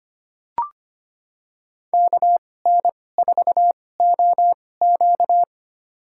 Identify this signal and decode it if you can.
Morse code, heard as a keyed tone, about 25 wpm, spaced as for 18 wpm, 700 Hz. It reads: KN4OQ
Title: USA Call Signs - 5 Characters - 2x Character Spacing - 25wpm